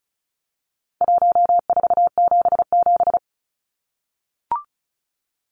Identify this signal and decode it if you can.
Morse code, heard as a keyed tone, about 34 wpm, 700 Hz. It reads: 1477
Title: Numbers - Contest Style - 4-Digit 35wpm